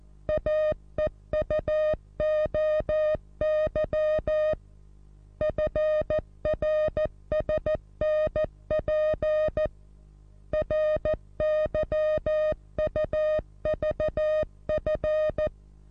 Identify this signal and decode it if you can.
Morse code, heard as a keyed tone, about 14 wpm, 620 Hz. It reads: AEUOY FRSNP RYUVF
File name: repos / Morsekurs / DJ4UF Morsen27